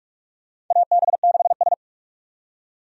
Morse code, heard as a keyed tone, about 46 wpm, 700 Hz. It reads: AB6S
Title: USA Call Signs - 4 Characters - 45wpm